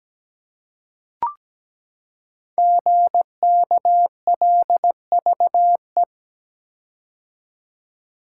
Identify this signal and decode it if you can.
Morse code, heard as a keyed tone, about 17 wpm, 700 Hz. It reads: GKLVE
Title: Code Groups - 5 Letter:Numbers - 17wpm